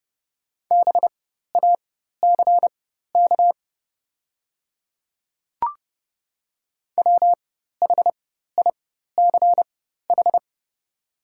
Morse code, heard as a keyed tone, about 31 wpm, spaced as for 14 wpm, 700 Hz. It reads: BACK WHICH
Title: 4x Character Spacing - Top 200 Words - 30wpm